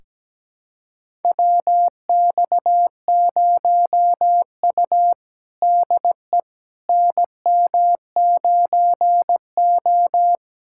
Morse code, heard as a keyed tone, about 17 wpm, 700 Hz. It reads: WX0U DE NM9O
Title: USA Callsign DE Callsign - 4 Characters - 17wpm